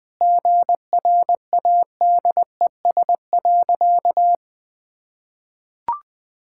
Morse code, heard as a keyed tone, about 20 wpm, 700 Hz. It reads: GRADES.
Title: Sentences with 4-Word Phrases - 20wpm